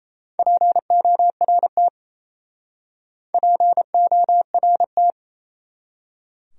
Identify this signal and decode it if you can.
Morse code, heard as a keyed tone, about 36 wpm, 700 Hz. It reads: PORT PORT